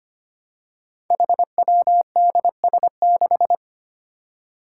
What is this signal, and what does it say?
Morse code, heard as a keyed tone, about 25 wpm, 700 Hz. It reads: HWDS6
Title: Five Challenging Letter-Number Combinations 25wpm